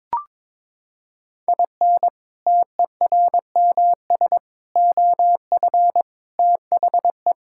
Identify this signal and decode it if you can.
Morse code, heard as a keyed tone, about 22 wpm, 700 Hz. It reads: IN TERMS OF THE
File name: practice / 22wpm / Sentences with 4-Word Phrases - 22wpm